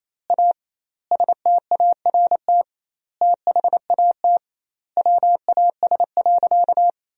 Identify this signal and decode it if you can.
Morse code, heard as a keyed tone, about 27 wpm, 700 Hz. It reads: A START THAT WAS.
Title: 2x Word-Spacing - Sentences from Top 300 Words - 28wpm